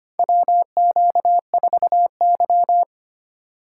Morse code, heard as a keyed tone, about 25 wpm, 700 Hz. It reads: WQ4Y
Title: USA Call Signs - 4 Characters - 3x Character Spacing - 25wpm